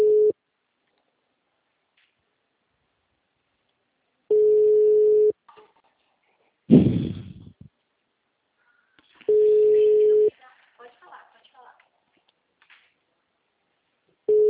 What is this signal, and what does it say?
Ringback tone (Europe, 425 Hz), with speech.
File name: out- -2009- - - .86305